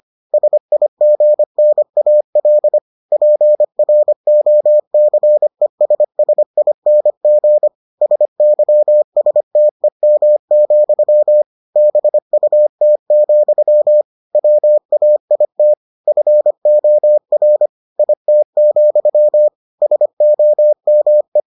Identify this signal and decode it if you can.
Morse code, heard as a keyed tone, about 25 wpm, 600 Hz. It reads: SIGNAL PROCESSING SYSTEM, BUT, WAIT FOR IT, SOME